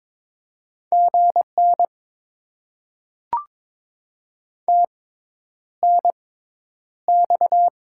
Morse code, heard as a keyed tone, about 22 wpm, spaced as for 10 wpm, 700 Hz. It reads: GN TNX
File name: Top 90 QSO Elements - 6x Character Spacing - 22wpm